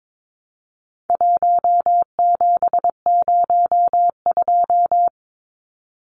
Morse code, heard as a keyed tone, about 22 wpm, 700 Hz. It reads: 1702
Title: Numbers - 4-Digit 22wpm